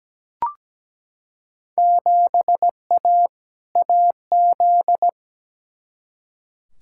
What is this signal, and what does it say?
Morse code, heard as a keyed tone, about 17 wpm, 700 Hz. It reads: 7A AZ
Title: Field Day Exchanges - 17wpm